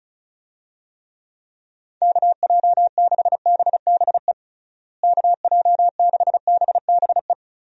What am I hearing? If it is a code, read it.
Morse code, heard as a keyed tone, about 36 wpm, 700 Hz. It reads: KJ6BBE KJ6BBE
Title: USA Call Signs - 6 Characters - Repeated 3x - 35wpm